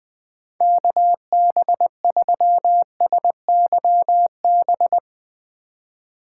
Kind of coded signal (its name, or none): Morse code